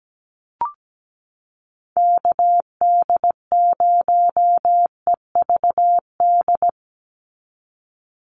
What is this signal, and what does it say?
Morse code, heard as a keyed tone, about 17 wpm, 700 Hz. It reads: KD0EVD
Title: USA Call Signs 17wpm